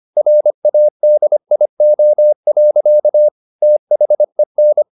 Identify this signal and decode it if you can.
Morse code, heard as a keyed tone, about 25 wpm, 600 Hz. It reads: RADIO. THEN
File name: the old man and the sea part 1 25wpm